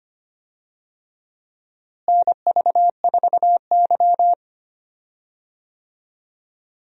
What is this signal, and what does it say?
Morse code, heard as a keyed tone, about 25 wpm, 700 Hz. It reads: NV4Y
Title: USA Call Signs - 4 Characters - Repeated 3x - 25wpm